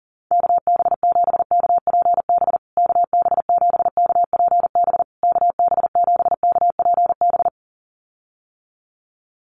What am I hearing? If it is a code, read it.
Morse code, heard as a keyed tone, about 37 wpm, 700 Hz. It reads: KB7KPB KB7KPB KB7KPB
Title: USA Call Signs Repeated 3x 40wpm